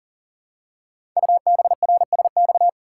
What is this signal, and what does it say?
Morse code, heard as a keyed tone, about 41 wpm, 690 Hz. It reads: UBRSX
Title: Code Groups - 5 Letters - 40wpm